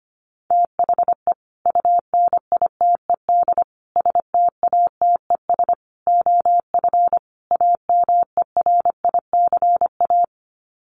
Morse code, heard as a keyed tone, about 25 wpm, 700 Hz. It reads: THE UNITED STATES OF AMERICA
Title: Sets of 5 Words 25wpm